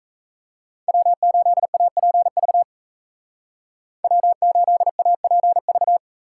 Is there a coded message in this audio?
Yes: Morse code from 0.9 to 6.0 s, heard as a keyed tone, about 40 wpm, 700 Hz. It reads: W8APV W8APV